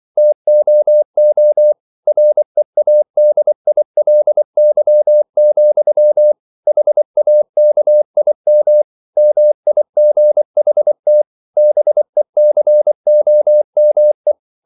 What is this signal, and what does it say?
Morse code, heard as a keyed tone, about 24 wpm, 600 Hz. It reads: TOO READILY, HAKIM MIGHT BECOME